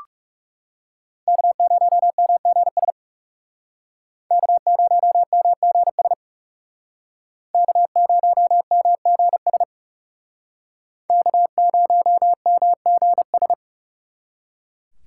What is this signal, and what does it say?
Morse code, heard as a keyed tone, about 36 wpm, 700 Hz. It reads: K0MGS K0MGS K0MGS K0MGS
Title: Speed Racing - USA 1x3 Call Signs 30wpm